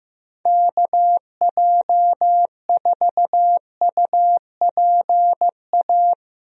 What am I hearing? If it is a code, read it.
Morse code, heard as a keyed tone, about 15 wpm, 700 Hz. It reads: KJ4UPA
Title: USA Call Signs - 6 Characters - Repeated 2x - 15wpm